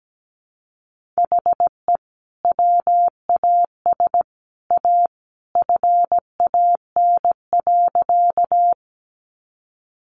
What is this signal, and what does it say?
Morse code, heard as a keyed tone, about 17 wpm, 700 Hz. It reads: HE WAS A FAN.
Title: Sentences from Top 1300 Words 17wpm